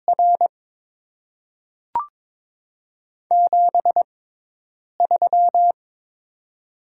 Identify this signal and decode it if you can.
Morse code, heard as a keyed tone, about 22 wpm, spaced as for 8 wpm, 700 Hz. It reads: R 73
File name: Top 20 QSO Elements - 6x Character Spacing - 22wpm